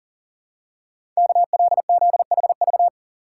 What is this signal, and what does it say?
Morse code, heard as a keyed tone, about 41 wpm, 700 Hz. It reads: KLZHV
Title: Code Groups - 5 Letters - 40wpm